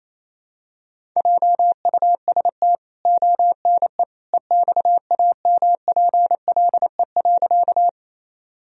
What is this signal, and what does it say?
Morse code, heard as a keyed tone, about 28 wpm, 700 Hz. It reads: JUST ONE EXAMPLE.
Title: Sentences from Top 1000 Words - Review Part 1 - 28wpm